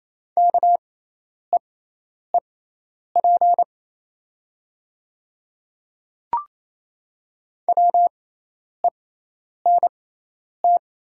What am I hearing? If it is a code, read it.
Morse code, heard as a keyed tone, about 28 wpm, spaced as for 10 wpm, 700 Hz. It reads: KEEP WENT
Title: 6x Character Spacing - Top 300 Words - 28wpm